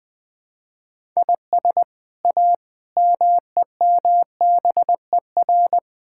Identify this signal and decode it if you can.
Morse code, heard as a keyed tone, about 20 wpm, 700 Hz. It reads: IS A MEMBER
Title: Build A Phrase - 5-Word Phrases - 20wpm